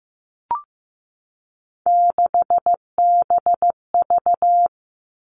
Morse code, heard as a keyed tone, about 15 wpm, 700 Hz. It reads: 6BV
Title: Three Challenging Letter-Number Combinations 15wpm